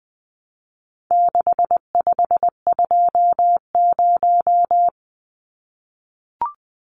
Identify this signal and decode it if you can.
Morse code, heard as a keyed tone, about 20 wpm, 700 Hz. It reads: 6520
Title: Numbers - 4-Digit 20wpm